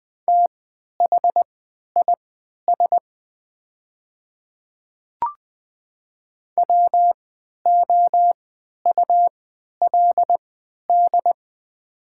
Morse code, heard as a keyed tone, about 20 wpm, spaced as for 11 wpm, 700 Hz. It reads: THIS WOULD
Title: 3x Character Spacing - Top 100 Words - 20wpm